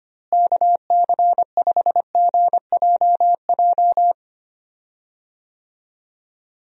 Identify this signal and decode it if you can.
Morse code, heard as a keyed tone, about 25 wpm, 700 Hz. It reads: KC5GJJ
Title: USA Call Signs - 6 Characters - Repeated 2x - 25wpm